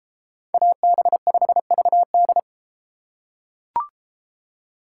Morse code, heard as a keyed tone, about 32 wpm, 700 Hz. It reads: AB5VD